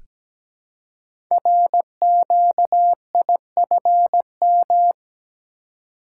Morse code, heard as a keyed tone, about 17 wpm, 700 Hz. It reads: RQIFM
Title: Code Groups - 5 Letters - 17wpm